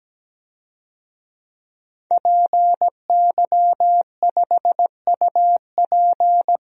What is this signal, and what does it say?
Morse code, heard as a keyed tone, about 17 wpm, 700 Hz. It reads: PY5UP
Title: DXCC-100 Call Signs - 5 Characters - Repeated 3x - 17wpm